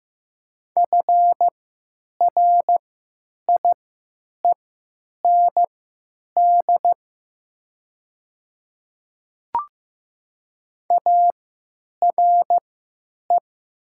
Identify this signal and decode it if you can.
Morse code, heard as a keyed tone, about 15 wpm, spaced as for 9 wpm, 700 Hz. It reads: FRIEND ARE
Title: 3x Character Spacing - Top 300 Words - 15wpm